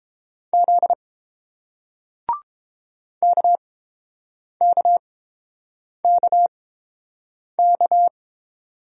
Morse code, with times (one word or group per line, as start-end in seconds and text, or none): 0.5-0.9 Z
3.2-8.1 KKKK